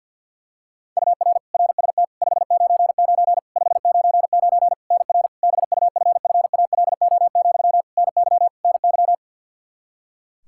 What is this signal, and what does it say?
Morse code, heard as a keyed tone, about 50 wpm, 700 Hz. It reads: UR RST 599 599 NR BUFFALO, NY NY
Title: K1USN SST QSO Practice 50wpm